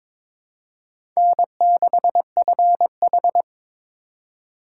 Morse code, heard as a keyed tone, about 21 wpm, 700 Hz. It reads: N6FH
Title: USA Call Signs - 4 Characters - 8x Character Spacing - 22wpm